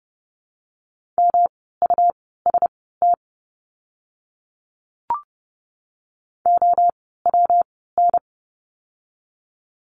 Morse code, heard as a keyed tone, about 30 wpm, spaced as for 17 wpm, 700 Hz. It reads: MUST OWN
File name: course / 30:10wpm / 237 Lesson 38f - Letter q - Words 30wpm